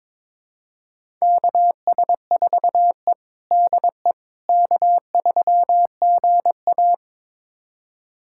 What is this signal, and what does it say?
Morse code, heard as a keyed tone, about 22 wpm, 700 Hz. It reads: KS4E DE K3GA
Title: USA Callsign DE Callsign - 4 Characters - 22wpm